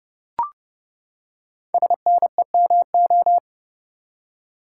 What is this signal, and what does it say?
Morse code, heard as a keyed tone, about 30 wpm, 700 Hz. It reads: SNEMO
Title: Code Groups - 5 Letters - 30wpm